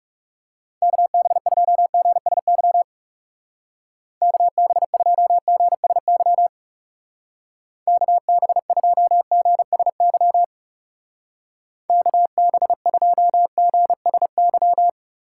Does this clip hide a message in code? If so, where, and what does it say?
Yes: Morse code from 0.8 to 14.9 s, heard as a keyed tone, about 38 wpm, 700 Hz. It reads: KB2GSY KB2GSY KB2GSY KB2GSY